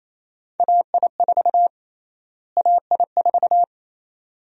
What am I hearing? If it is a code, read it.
Morse code, heard as a keyed tone, about 28 wpm, 700 Hz. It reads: AI4 AI4
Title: USA Call Sign Prefixes - Repeated 2x - 28wpm